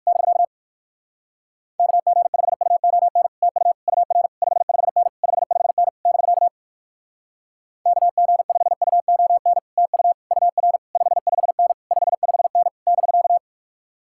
Morse code, heard as a keyed tone, about 60 wpm, 700 Hz. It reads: <BK> KG5UON TU UR 55N 55N <BK> KG5UON TU UR 55N 55N <BK>